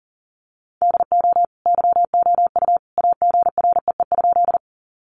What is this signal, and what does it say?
Morse code, heard as a keyed tone, about 40 wpm, 700 Hz. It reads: DO YOU AGREE?
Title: Sentences from Top 1100 Words 40wpm